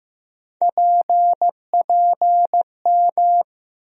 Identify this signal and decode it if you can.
Morse code, heard as a keyed tone, about 15 wpm, 700 Hz. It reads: PPM
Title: USA Call Sign Postfixes - 3 Letters 15wpm